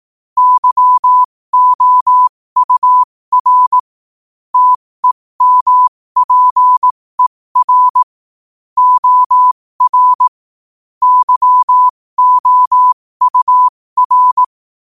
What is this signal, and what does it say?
Morse code, heard as a keyed tone, about 18 wpm, spaced as for 15 wpm, 1000 Hz. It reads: YOUR TEMPER OR YOUR
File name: qotdfiles / 15wpm